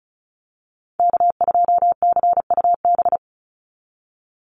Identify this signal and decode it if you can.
Morse code, heard as a keyed tone, about 34 wpm, 700 Hz. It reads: K2CUB
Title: Call Signs Repeated 3x - 35wpm